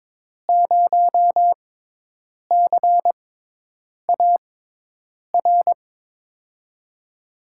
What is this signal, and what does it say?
Morse code, heard as a keyed tone, about 22 wpm, spaced as for 8 wpm, 700 Hz. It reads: 0CAR
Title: USA Call Signs - 6 Characters - 6x Character Spacing - 22wpm